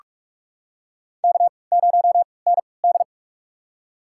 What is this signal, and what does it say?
Morse code, heard as a keyed tone, about 46 wpm, spaced as for 26 wpm, 690 Hz. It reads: K0ND
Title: USA Call Signs - 4 Characters - 3x Character Spacing - 45wpm